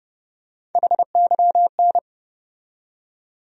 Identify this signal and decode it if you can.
Morse code, heard as a keyed tone, about 31 wpm, 700 Hz. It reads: HYN
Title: USA Call Sign Postfixes - 3 Letters 30wpm